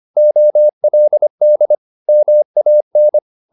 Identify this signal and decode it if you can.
Morse code, heard as a keyed tone, about 25 wpm, 600 Hz. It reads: OLD MAN
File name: the old man and the sea part 1 25wpm